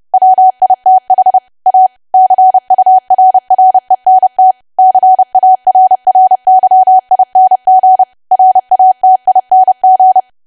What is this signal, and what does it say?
Morse code, heard as a keyed tone, about 30 wpm, 750 Hz. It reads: WITH A CURRENT CARRYING RATING
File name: cwa / W1AW-Level3 / 30WPM